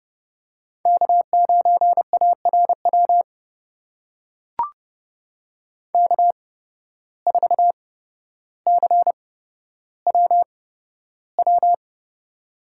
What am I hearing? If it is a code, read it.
Morse code, heard as a keyed tone, about 30 wpm, spaced as for 13 wpm, 700 Hz. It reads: K9ARW K4CWW
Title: USA Call Signs - 5 Characters - 8x Character Spacing - 30wpm